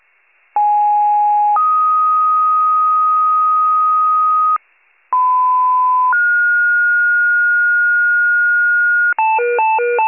Two-tone page, heard as a one-second tone, then a three-second tone, about 830 then 1250 Hz, about 1010 then 1430 Hz, with noise. It is heard from 0.6 to 9.1 s.